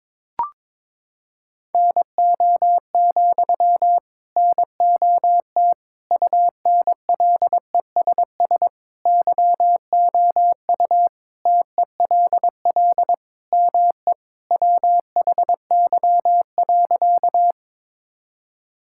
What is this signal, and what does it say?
Morse code, heard as a keyed tone, about 21 wpm, 700 Hz. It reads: NO, NOT UNLESS YOU TELL ME WHY.
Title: Sentences from Top 1000 Words - Review Part 6 - 22wpm